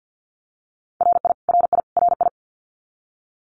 Morse code, heard as a keyed tone, about 50 wpm, 700 Hz. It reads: RI RI RI
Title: US States Repeated 3x 50wpm